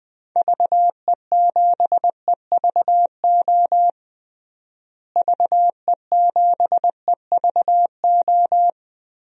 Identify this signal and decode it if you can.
Morse code, heard as a keyed tone, about 21 wpm, 700 Hz. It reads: VE7EVO VE7EVO
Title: DXCC-100 Call Signs - 6 Characters - Repeated 2x - 20wpm